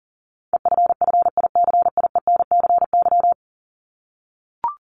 Morse code, heard as a keyed tone, about 40 wpm, 700 Hz. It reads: EFFICIENCY
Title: Top 1501-1750 Words 40wpm